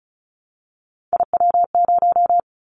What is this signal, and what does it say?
Morse code, heard as a keyed tone, about 36 wpm, 700 Hz. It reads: IW0
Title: Callsign Prefixes for Top 25 DXCC - 35wpm